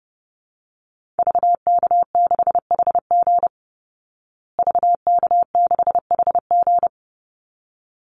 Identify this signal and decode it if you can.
Morse code, heard as a keyed tone, about 30 wpm, 700 Hz. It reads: VK6HG VK6HG
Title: Call Signs Repeated 3x - 30wpm